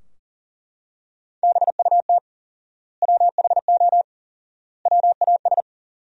Morse code, heard as a keyed tone, about 40 wpm, 700 Hz. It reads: BUT WHO WAS
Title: 4x Word-Spacing - Sentences from Top 200 Words - 40wpm